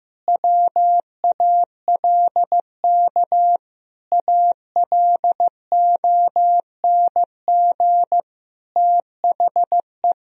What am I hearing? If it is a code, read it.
Morse code, heard as a keyed tone, about 15 wpm, 700 Hz. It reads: WALK ALONG THE
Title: Sentences from Top 1000 Words - Review Part 4 - 15wpm